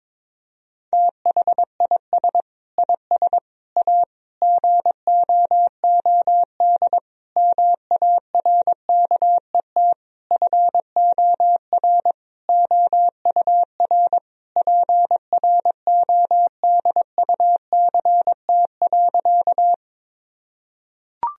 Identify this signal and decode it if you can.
Morse code, heard as a keyed tone, about 22 wpm, 700 Hz. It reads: THIS IS A GOOD MARKET FOR OUR PRODUCT.
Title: Sentences from Top 1000 Words - Review Part 3 - 22wpm